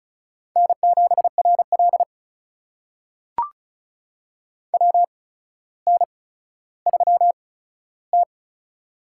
Morse code, heard as a keyed tone, about 36 wpm, spaced as for 15 wpm, 700 Hz. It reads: N7RL WN3T
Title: USA Call Signs - 4 Characters - 8x Character Spacing - 35wpm